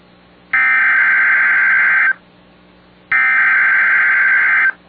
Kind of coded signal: Emergency Alert System (SAME) tones